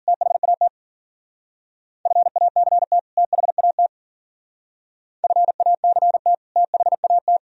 Morse code, heard as a keyed tone, about 47 wpm, 690 Hz. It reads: THAT FACT THAT FACT THAT
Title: Speed-Racing - Sets of 2 Words 40wpm